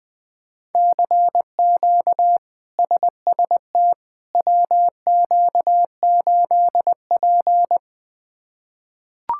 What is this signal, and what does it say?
Morse code, heard as a keyed tone, about 20 wpm, 700 Hz. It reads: CQ SST WQ8P
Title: K1USN SST QSO Practice - 20wpm